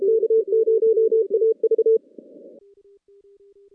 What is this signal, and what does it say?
Morse code, heard as a keyed tone, about 32 wpm, 450 Hz. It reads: K0AV
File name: contests / CW-SS-k5tr- -2200